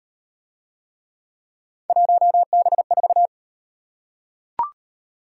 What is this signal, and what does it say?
Morse code, heard as a keyed tone, about 38 wpm, 700 Hz. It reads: 1B4